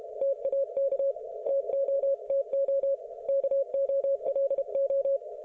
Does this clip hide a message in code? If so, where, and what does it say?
Yes: Morse code from 0.2 to 5.2 s, heard as a keyed tone, about 32 wpm, 550 Hz. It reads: TAK TOTO KOLO